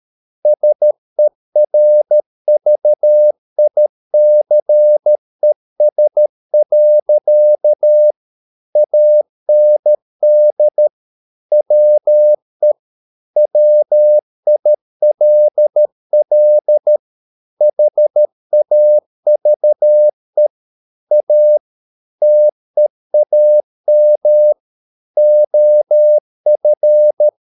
Morse code, heard as a keyed tone, about 13 wpm, 600 Hz. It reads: SERVICES. AND WE WILL HAVE A TEAM OF